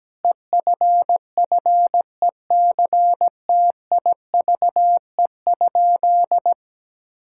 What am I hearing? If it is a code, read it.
Morse code, heard as a keyed tone, about 17 wpm, 700 Hz. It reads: EFFECTIVE?
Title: Sentences from Top 1000 Words - Review Part 6 - 17wpm